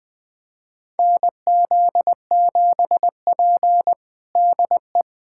Morse code, heard as a keyed tone, about 20 wpm, 700 Hz. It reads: NZ7P DE
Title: USA Callsign DE Callsign - 4 Characters - 20wpm